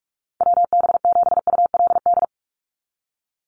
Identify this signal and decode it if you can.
Morse code, heard as a keyed tone, about 44 wpm, 690 Hz. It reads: WB7ULD
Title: NA SOTA Chaser Callsigns 45wpm